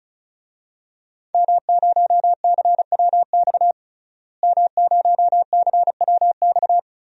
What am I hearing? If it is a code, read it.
Morse code, heard as a keyed tone, about 36 wpm, 700 Hz. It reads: M0CWX M0CWX
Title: DXCC-100 Call Signs - 5 Characters - Repeated 3x - 35wpm